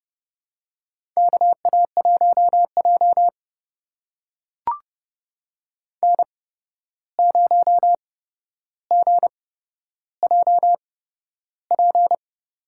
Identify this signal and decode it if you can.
Morse code, heard as a keyed tone, about 30 wpm, spaced as for 12 wpm, 700 Hz. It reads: KA1J N0GJP
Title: USA Call Signs - 8x Character Spacing - 30wpm